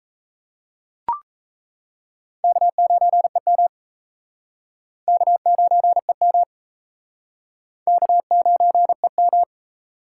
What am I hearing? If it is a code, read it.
Morse code, heard as a keyed tone, about 39 wpm, 700 Hz. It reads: K9EM K9EM K9EM